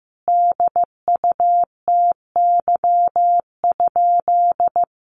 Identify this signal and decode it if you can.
Morse code, heard as a keyed tone, about 14 wpm, 700 Hz. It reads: DUTY?
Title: Sentences from Top 1500 Words 15wpm